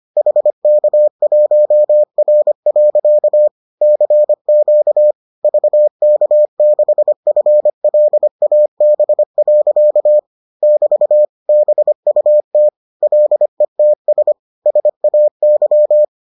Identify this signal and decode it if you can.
Morse code, heard as a keyed tone, about 25 wpm, 600 Hz. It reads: HK1R. CQ VK6FLAB. <BT> BUT LETS SAY